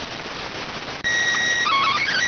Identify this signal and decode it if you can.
SSTV (slow-scan television), PD 290.